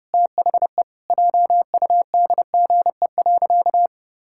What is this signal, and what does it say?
Morse code, heard as a keyed tone, about 30 wpm, 700 Hz. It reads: THE JUDGE.
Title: Pangram Challenge v2 - 30wpm